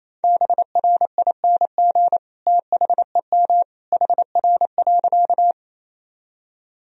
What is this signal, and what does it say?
Morse code, heard as a keyed tone, about 28 wpm, 700 Hz. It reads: BRING THEM HR.
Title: Sentences from Top 500 Words and Abbreviations 28wpm